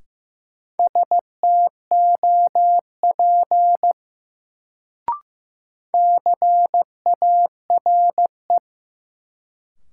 Morse code, heard as a keyed tone, about 15 wpm, 700 Hz. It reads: STOP CARE
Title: Sentences from Top 300 Words - New Word Drill - 15wpm